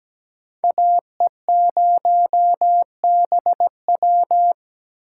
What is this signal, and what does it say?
Morse code, heard as a keyed tone, about 17 wpm, 700 Hz. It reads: AE0BW
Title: USA Call Signs - 5 Characters - 4x Character Spacing - 17wpm